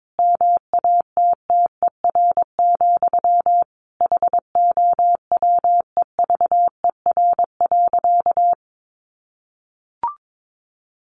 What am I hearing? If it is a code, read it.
Morse code, heard as a keyed tone, about 22 wpm, 700 Hz. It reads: MATTER, HOWEVER.
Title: Sentences from Top 700 Words 22wpm